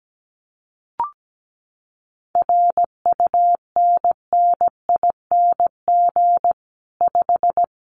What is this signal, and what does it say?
Morse code, heard as a keyed tone, about 17 wpm, 700 Hz. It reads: RUNNING 5
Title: Common QSO Phrases 17wpm